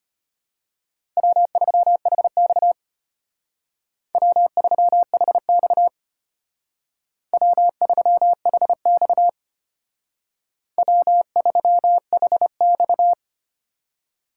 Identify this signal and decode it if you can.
Morse code, heard as a keyed tone, about 31 wpm, 700 Hz. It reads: W3HX W3HX W3HX W3HX
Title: Speed Racing - USA 1x2 Call Signs 25wpm